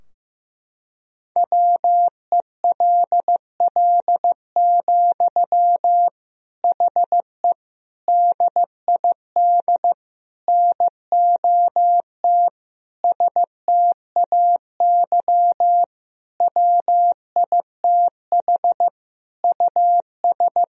Morse code, heard as a keyed tone, about 15 wpm, 700 Hz. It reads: WELL, HE DID NOT STAY WITH US
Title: Sentences from Top 1000 Words - Review Part 2 - 15wpm